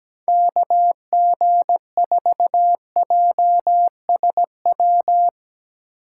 Morse code, heard as a keyed tone, about 17 wpm, 700 Hz. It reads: KG4JSW
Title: USA Call Signs - 6 Characters - 4x Character Spacing - 17wpm